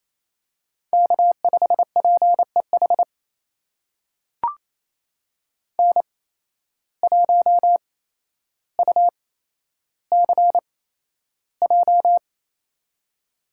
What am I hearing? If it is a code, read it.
Morse code, heard as a keyed tone, about 28 wpm, spaced as for 12 wpm, 700 Hz. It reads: K5PEH N1UCJ
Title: USA Call Signs - 5 Characters - 8x Character Spacing - 28wpm